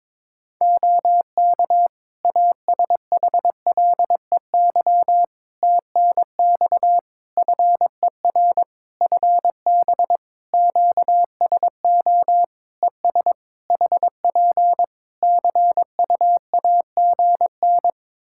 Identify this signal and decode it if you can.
Morse code, heard as a keyed tone, about 22 wpm, 700 Hz. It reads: OK ASHLEY TNX FER FB QSO ES HP CUAGN
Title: Practice QSO - Exchange 4 - Ending - 22wpm